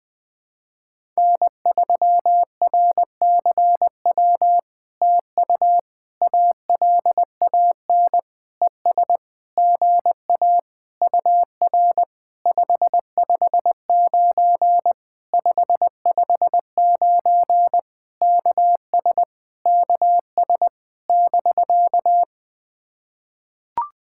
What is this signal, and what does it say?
Morse code, heard as a keyed tone, about 20 wpm, 700 Hz. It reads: N3RCW TU ALAN ES GA UR 559 559 KS KS <BK>
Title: POTA QSO Practice - 20wpm